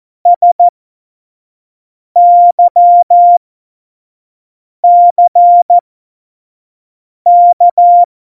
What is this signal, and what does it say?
Morse code, heard as a keyed tone, about 14 wpm, spaced as for 5 wpm, 700 Hz. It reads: SYCK
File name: Morsekurs / Witze / Witz 05. Juni 2018, 25 F70